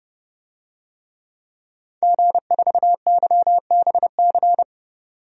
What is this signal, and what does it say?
Morse code, heard as a keyed tone, about 30 wpm, 700 Hz. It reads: G4YBC
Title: DXCC-100 Call Signs - 5 Characters - 30wpm